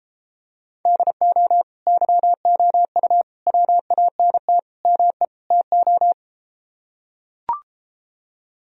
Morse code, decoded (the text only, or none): DO YOU WANT ME TO